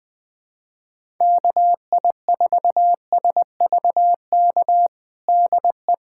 Morse code, heard as a keyed tone, about 21 wpm, 700 Hz. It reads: KI4SVK DE
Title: USA Callsign DE Callsign - 6 Characters - 20wpm